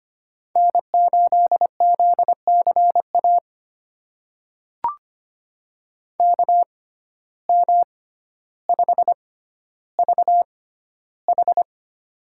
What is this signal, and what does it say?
Morse code, heard as a keyed tone, about 25 wpm, spaced as for 13 wpm, 700 Hz. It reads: N8ZCA KM5VH
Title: USA Call Signs - 5 Characters - 6x Character Spacing - 25wpm